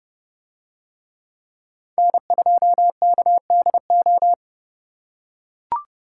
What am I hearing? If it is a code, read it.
Morse code, heard as a keyed tone, about 29 wpm, 700 Hz. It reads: N2KDO